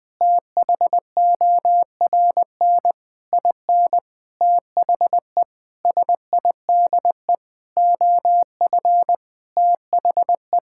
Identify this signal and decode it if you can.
Morse code, heard as a keyed tone, about 20 wpm, 700 Hz. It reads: THORN IN THE SIDE OF THE
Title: Sentences with 8-Word Phrases - 20wpm